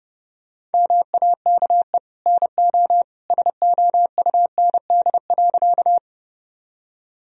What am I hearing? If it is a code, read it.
Morse code, heard as a keyed tone, about 32 wpm, 700 Hz. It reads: MAKE NO SOUND.
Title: Sentences from Top 1000 Words - Review Part 1 - 30wpm